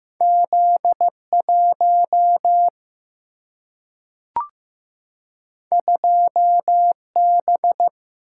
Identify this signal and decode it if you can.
Morse code, heard as a keyed tone, about 15 wpm, 700 Hz. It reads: Z1 2B
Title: Single-Letter Single-Number 15wpm